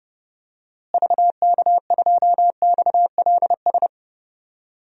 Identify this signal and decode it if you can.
Morse code, heard as a keyed tone, about 32 wpm, 700 Hz. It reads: VK2XLS
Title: DXCC-100 Call Signs - 6 Characters - Repeated 3x - 30wpm